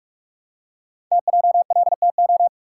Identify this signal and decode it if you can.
Morse code, heard as a keyed tone, about 44 wpm, 700 Hz. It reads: TJLTO
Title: Code Groups - 5 Letters - 45wpm